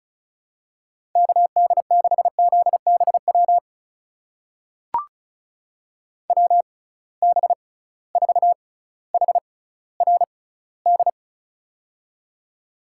Morse code, heard as a keyed tone, about 36 wpm, spaced as for 18 wpm, 700 Hz. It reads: KD6ZBW WB4HRD
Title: USA Call Signs - 6 Characters - 6x Character Spacing - 35wpm